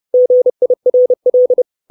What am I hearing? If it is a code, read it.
Morse code, heard as a keyed tone, about 30 wpm, 500 Hz. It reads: GIRL
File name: 632 Words 30WPM